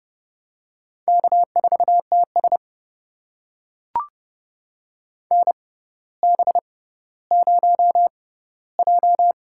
Morse code, heard as a keyed tone, about 30 wpm, spaced as for 15 wpm, 700 Hz. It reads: K4TS NB0J
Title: USA Call Signs - 4 Characters - 6x Character Spacing - 30wpm